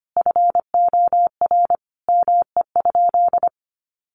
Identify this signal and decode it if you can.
Morse code, heard as a keyed tone, about 25 wpm, 700 Hz. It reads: FOR ME?
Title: Sentences from Top 1500 Words 25wpm